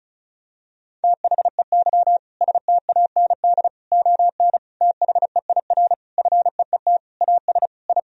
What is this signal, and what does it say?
Morse code, heard as a keyed tone, about 36 wpm, 700 Hz. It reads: THEY STAND ON THEIR FEET AS I
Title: Sentences from Top 1000 Words - Review Part 1 - 35wpm